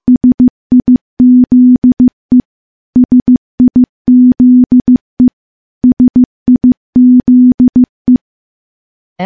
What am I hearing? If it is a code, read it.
Morse code, heard as a keyed tone, about 15 wpm, 250 Hz. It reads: SIZE SIZE SIZE
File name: morse / morse-training-15WPM